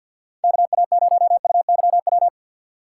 Morse code, heard as a keyed tone, about 65 wpm, 700 Hz. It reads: KA0UYW